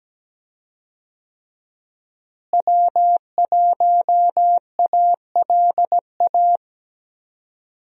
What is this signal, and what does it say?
Morse code, heard as a keyed tone, about 17 wpm, 700 Hz. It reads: W1ALA
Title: USA Call Signs - 5 Characters - Repeated 3x - 17wpm